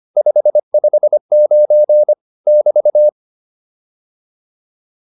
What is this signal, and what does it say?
Morse code, heard as a keyed tone, about 25 wpm, 600 Hz. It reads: 559 <BT>
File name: QSO103 25